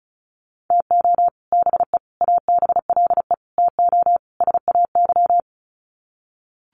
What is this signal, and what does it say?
Morse code, heard as a keyed tone, about 34 wpm, 700 Hz. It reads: TO BE ABLE TO SAY
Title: Sets of 5 Words Encore2 35wpm